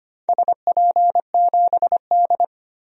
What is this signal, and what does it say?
Morse code, heard as a keyed tone, about 25 wpm, 700 Hz. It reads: SP7D
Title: DXCC-100 Call Signs - 4 Characters - Repeated 3x - 25wpm